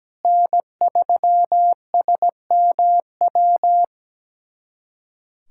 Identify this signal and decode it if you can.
Morse code, heard as a keyed tone, about 17 wpm, 700 Hz. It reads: N3SMW
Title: USA Callsign DE Callsign - 5 Characters - 17wpm